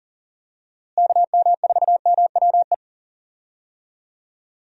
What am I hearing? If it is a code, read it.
Morse code, heard as a keyed tone, about 40 wpm, 700 Hz. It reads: KM4MWE
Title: USA Call Signs - 6 Characters - Repeated 3x - 40wpm